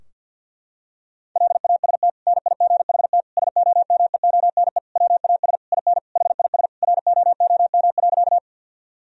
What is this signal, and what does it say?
Morse code, heard as a keyed tone, about 50 wpm, 700 Hz. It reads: LAST NIGHT SOMEONE WAS IN HIS ROOM.
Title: Sentences from Top 1000 Words - Review Part 4 - 50wpm